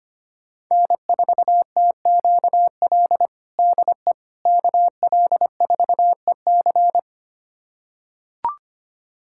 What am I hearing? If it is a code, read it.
Morse code, heard as a keyed tone, about 25 wpm, 700 Hz. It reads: N4TQL DE KL4EC
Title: USA Callsign DE Callsign - 5 Characters - 25wpm